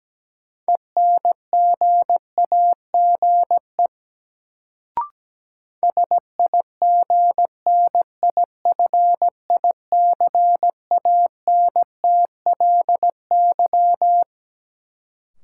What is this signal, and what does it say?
Morse code, heard as a keyed tone, about 17 wpm, 700 Hz. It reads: ENGAGE SIGNIFICANTLY
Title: Sentences from Top 1500 Words - New Word Drill Alt - 17wpm